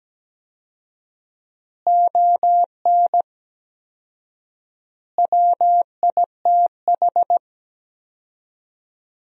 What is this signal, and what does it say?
Morse code, heard as a keyed tone, about 17 wpm, 700 Hz. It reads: ON WITH
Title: 4x Word-Spacing - Sentences from Top 300 Words - 17wpm